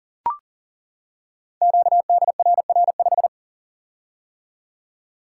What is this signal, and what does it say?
Morse code, heard as a keyed tone, about 40 wpm, 700 Hz. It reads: QDRR5
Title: Code Groups - 5 Letter:Numbers - 40wpm